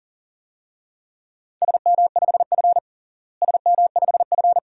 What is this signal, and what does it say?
Morse code, heard as a keyed tone, about 40 wpm, 700 Hz. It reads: SM5F SM5F
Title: DXCC-100 Call Signs - 4 Characters - Repeated 2x - 40wpm